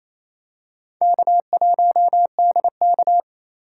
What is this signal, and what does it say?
Morse code, heard as a keyed tone, about 28 wpm, 700 Hz. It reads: K1DK
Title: USA Call Signs - 4 Characters - 4x Character Spacing - 28wpm